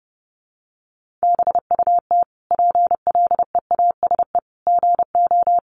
Morse code, heard as a keyed tone, about 30 wpm, 700 Hz. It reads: BUT PLEASE GO
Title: Sentences from Top 1500 Words 30wpm